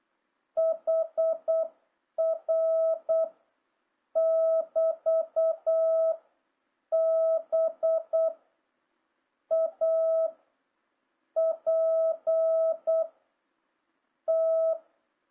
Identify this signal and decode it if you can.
Morse code, heard as a keyed tone, about 8 wpm, 650 Hz. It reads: HR <BT> B A P T
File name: jh8cbh / 2026cw